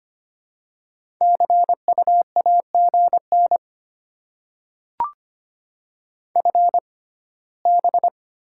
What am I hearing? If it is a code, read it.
Morse code, heard as a keyed tone, about 25 wpm, spaced as for 17 wpm, 700 Hz. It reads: CU AGN FB
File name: Top 20 QSO Elements - 6x Character Spacing - 25wpm